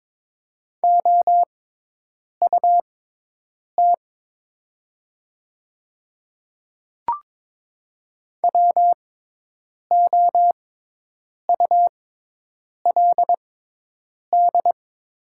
Morse code, heard as a keyed tone, about 22 wpm, spaced as for 8 wpm, 690 Hz. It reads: OUT WOULD
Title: 6x Character Spacing - Top 100 Words - 22wpm